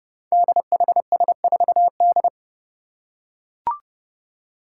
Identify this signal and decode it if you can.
Morse code, heard as a keyed tone, about 30 wpm, 700 Hz. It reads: DHS4D
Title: Five Challenging Letter-Number Combinations 30wpm